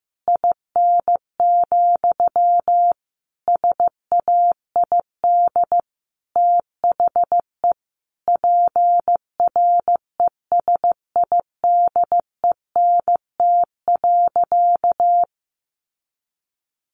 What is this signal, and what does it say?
Morse code, heard as a keyed tone, about 15 wpm, 700 Hz. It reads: IN, SAID THE PRESIDENT.
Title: Sentences from Top 1000 Words 15wpm